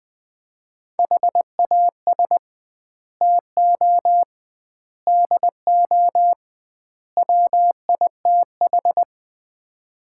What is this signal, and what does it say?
Morse code, heard as a keyed tone, about 21 wpm, 700 Hz. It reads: HAS TO DO WITH
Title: 2x Word-Spacing - Sets of 4 Words - 20wpm